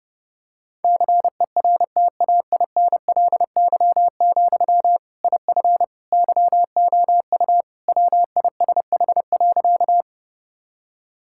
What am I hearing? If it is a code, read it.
Morse code, heard as a keyed tone, about 30 wpm, 700 Hz. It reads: CERTAINLY, IF YOU WISH.
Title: Sentences from Top 1600 Words 30wpm